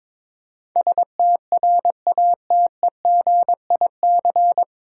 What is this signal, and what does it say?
Morse code, heard as a keyed tone, about 22 wpm, 700 Hz. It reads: STRATEGIC
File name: 9 Letter Words - First 3 Letters Given - 22wpm